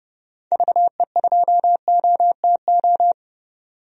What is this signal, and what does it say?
Morse code, heard as a keyed tone, about 32 wpm, 690 Hz. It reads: VE2OTO
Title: DXCC-100 Call Signs - 6 Characters - 30wpm